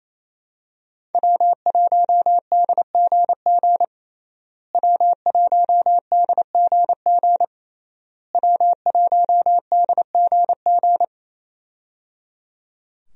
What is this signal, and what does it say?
Morse code, heard as a keyed tone, about 27 wpm, 700 Hz. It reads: W1DGG W1DGG W1DGG
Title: USA Call Signs - 5 Characters - Repeated 3x - 28wpm